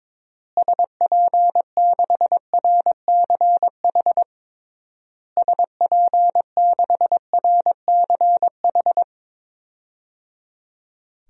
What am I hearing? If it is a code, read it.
Morse code, heard as a keyed tone, about 22 wpm, 700 Hz. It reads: SP6RCH SP6RCH
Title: DXCC-100 Call Signs - 6 Characters - Repeated 3x - 22wpm